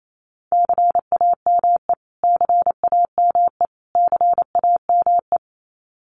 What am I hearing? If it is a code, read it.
Morse code, heard as a keyed tone, about 28 wpm, 700 Hz. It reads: CAME CAME CAME
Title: Top 300 Words Repeated 28wpm